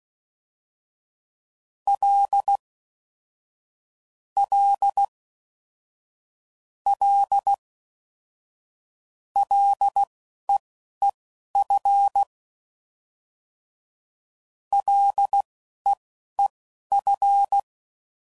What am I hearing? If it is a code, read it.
Morse code, heard as a keyed tone, about 16 wpm, spaced as for 11 wpm, 800 Hz. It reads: L L L LEEF LEEF
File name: Basic Sets / QST de WA7RSO / Combnations